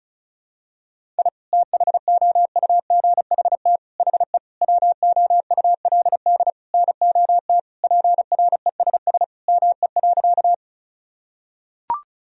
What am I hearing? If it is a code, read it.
Morse code, heard as a keyed tone, about 34 wpm, 700 Hz. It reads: I THOUGHT HE WOULD NOT PRESS ME.
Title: Sentences from Top 1000 Words - Review Part 1 - 35wpm